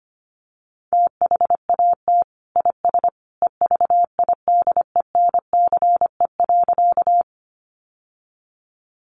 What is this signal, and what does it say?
Morse code, heard as a keyed tone, about 25 wpm, 700 Hz. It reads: THAT IS EVIDENCE.
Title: Sentences from Top 1300 Words 25wpm